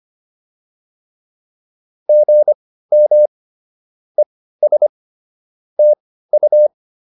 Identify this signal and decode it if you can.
Morse code, heard as a keyed tone, about 25 wpm, spaced as for 15 wpm, 600 Hz. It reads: GM ES TU